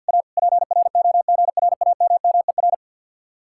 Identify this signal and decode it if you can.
Morse code, heard as a keyed tone, about 62 wpm, 700 Hz. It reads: A PROGRAMMER
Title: Speed-Racing Common QSO Phrases - 50wpm